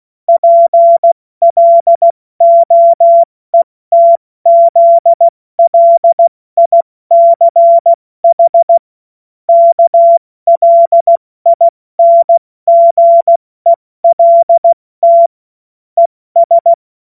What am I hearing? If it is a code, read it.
Morse code, heard as a keyed tone, about 16 wpm, 680 Hz. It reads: PLOETZLICH KLINGELT ES
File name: Morsekurs / Witze / Witz 16. November 2021, 70 F80